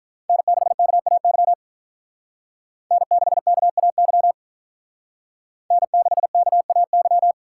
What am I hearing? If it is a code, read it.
Morse code, heard as a keyed tone, about 42 wpm, 700 Hz. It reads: N6KAY N6KAY N6KAY
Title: Speed Racing - USA Call Signs 35wpm